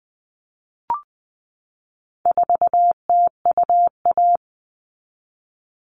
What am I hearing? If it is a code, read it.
Morse code, heard as a keyed tone, about 20 wpm, 700 Hz. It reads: 4TUA
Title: All Cut Numbers - 4-Digit 20wpm